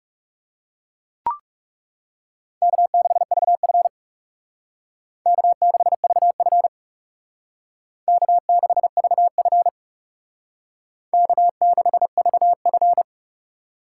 Morse code, heard as a keyed tone, about 38 wpm, 700 Hz. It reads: K6VF K6VF K6VF K6VF